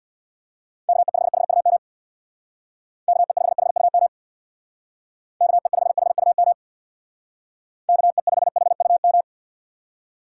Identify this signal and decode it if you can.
Morse code, heard as a keyed tone, about 65 wpm, 700 Hz. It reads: KE5HUM KE5HUM KE5HUM KE5HUM